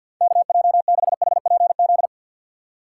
Morse code, heard as a keyed tone, about 47 wpm, 700 Hz. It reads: KJ6HPZ
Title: USA Call Signs - 6 Characters - 8x Character Spacing - 50wpm